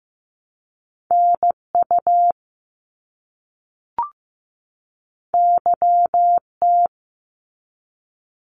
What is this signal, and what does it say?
Morse code, heard as a keyed tone, about 15 wpm, 700 Hz. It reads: NU YT
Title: Canadian Provinces and Territories 15wpm